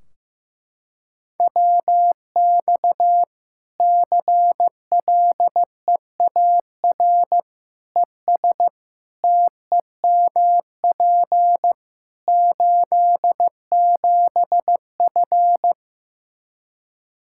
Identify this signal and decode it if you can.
Morse code, heard as a keyed tone, about 15 wpm, 700 Hz. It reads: WX CLEAR ES TEMP 87F
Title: Practice QSO - Exchange 2 - Info - 15wpm